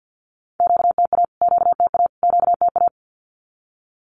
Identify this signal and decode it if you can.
Morse code, heard as a keyed tone, about 50 wpm, 700 Hz. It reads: QTA QTA QTA
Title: Q-Codes Repeated 3x 50wpm